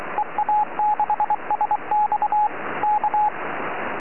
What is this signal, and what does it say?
Morse code, heard as a keyed tone, about 23 wpm, 840 Hz, with radio noise. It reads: EA6SX K
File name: ea6sx 4m